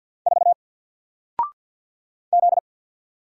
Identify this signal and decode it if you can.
Morse code, heard as a keyed tone, about 50 wpm, 700 Hz. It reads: 4 Z
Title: Single Letter-Number Alt 50wpm